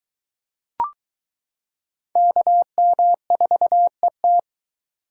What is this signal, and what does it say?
Morse code, heard as a keyed tone, about 22 wpm, 700 Hz. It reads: KM4ET